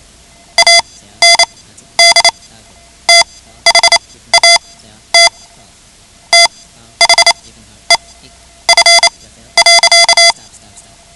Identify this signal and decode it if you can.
Morse code, heard as a keyed tone, about 28 wpm, spaced as for 13 wpm, 2050 Hz. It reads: AND THAT THE F.
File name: ham radio / CodeQuick / Code Quick 10